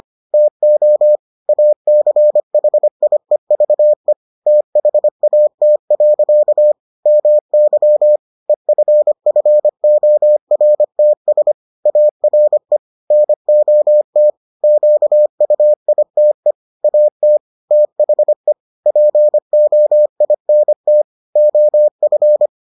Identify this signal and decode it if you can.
Morse code, heard as a keyed tone, about 25 wpm, 600 Hz. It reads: TO ACHIEVE THAT. MY EFFORTS ARE NOT QUITE AT THE POINT OF